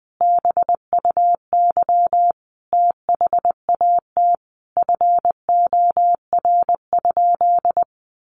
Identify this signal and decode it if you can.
Morse code, heard as a keyed tone, about 20 wpm, 700 Hz. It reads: BUY THAT FOR?
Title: Sentences from Top 700 Words 20wpm